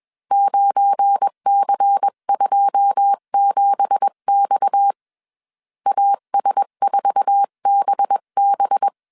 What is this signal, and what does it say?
Morse code, heard as a keyed tone, about 21 wpm, 800 Hz. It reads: 9C27X AS4BB
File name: CW / DET / 1 9 1500C S B 20wpm 2313